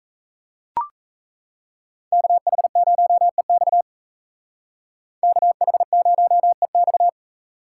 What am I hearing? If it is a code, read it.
Morse code, heard as a keyed tone, about 32 wpm, 700 Hz. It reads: KH0EX KH0EX
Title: Speed Racing - USA 2x2 Call Signs 28wpm